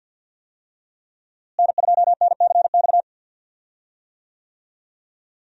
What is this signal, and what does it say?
Morse code, heard as a keyed tone, about 49 wpm, 700 Hz. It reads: N2NCX
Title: USA Call Signs - 5 Characters - Repeated 2x - 50wpm